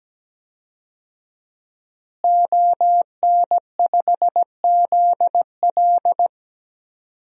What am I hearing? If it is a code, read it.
Morse code, heard as a keyed tone, about 17 wpm, 700 Hz. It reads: ON5ZL
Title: DXCC-100 Call Signs - 5 Characters - Repeated 2x - 17wpm